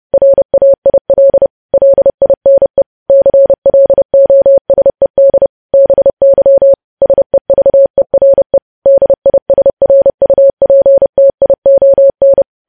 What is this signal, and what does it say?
Morse code, heard as a keyed tone, about 30 wpm, 550 Hz. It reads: RAIL LINE CLOSED BY SEVERE DISRUPTION
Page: news headlines in morse code at 30 wpm